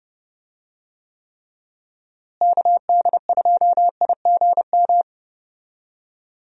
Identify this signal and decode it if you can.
Morse code, heard as a keyed tone, about 30 wpm, 700 Hz. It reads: KD2IGM
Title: USA Call Signs - 6 Characters - Repeated 2x - 30wpm